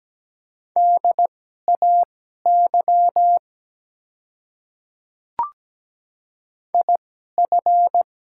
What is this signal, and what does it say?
Morse code, heard as a keyed tone, about 17 wpm, spaced as for 12 wpm, 700 Hz. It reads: DAY IF
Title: 2x Character Spacing - Top 100 Words - 17wpm